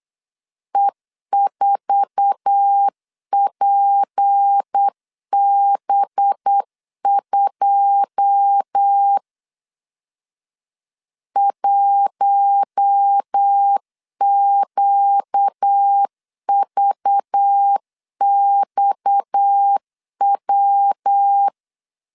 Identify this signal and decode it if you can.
Morse code, heard as a keyed tone, about 8 wpm, 800 Hz. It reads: E4PB2 1QVXW
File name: CW / DET / 1 9 1500C S B 8wpm 5624